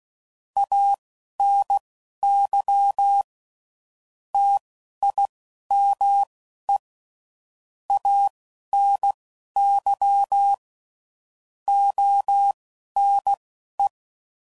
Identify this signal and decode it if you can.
Morse code, heard as a keyed tone, about 16 wpm, spaced as for 11 wpm, 800 Hz. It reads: ANY TIME ANY ONE
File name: CW Code Groups @10WPM / Technical Subjects / OR Gate